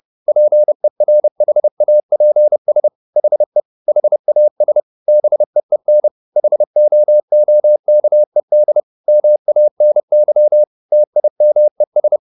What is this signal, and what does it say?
Morse code, heard as a keyed tone, about 30 wpm, 600 Hz. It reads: PERHAPS HE HAS BEEN HOOKED MANY TIMES
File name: the old man and the sea / the old man and the sea 1 30wpm